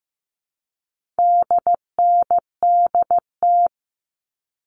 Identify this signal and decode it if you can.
Morse code, heard as a keyed tone, about 15 wpm, 700 Hz. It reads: DNDT
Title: All Cut Numbers - 4-Digit 15wpm